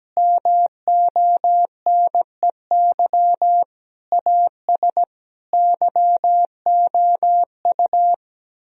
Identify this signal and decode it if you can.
Morse code, heard as a keyed tone, about 17 wpm, 700 Hz. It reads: MONEY AS YOU
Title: Sentences from Top 1000 Words - Review Part 3 - 17wpm